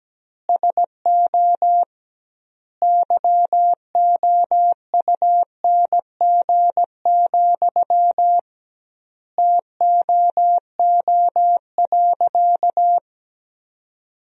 Morse code, heard as a keyed tone, about 17 wpm, 700 Hz. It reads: SO YOUNG, TOO.
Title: 2x Word-Spacing - Sentences from Top 400 Words - 17wpm